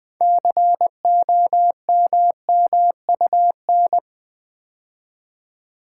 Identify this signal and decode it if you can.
Morse code, heard as a keyed tone, about 20 wpm, 700 Hz. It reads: COMMUN
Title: Build A Word - 3 Letters At A Time - 9-Letter Words - 20wpm